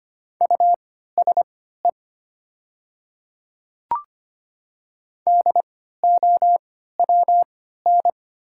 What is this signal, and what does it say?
Morse code, heard as a keyed tone, about 25 wpm, spaced as for 14 wpm, 700 Hz. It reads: USE DOWN
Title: 3x Character Spacing - Top 300 Words - 25wpm